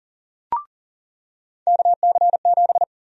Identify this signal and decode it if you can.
Morse code, heard as a keyed tone, about 40 wpm, 700 Hz. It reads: KC7
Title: USA Call Sign Prefixes - Repeated 2x - 40wpm